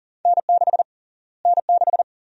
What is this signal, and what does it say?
Morse code, heard as a keyed tone, about 40 wpm, 700 Hz. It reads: N6 N6
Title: USA Call Sign Prefixes - Repeated 2x - 40wpm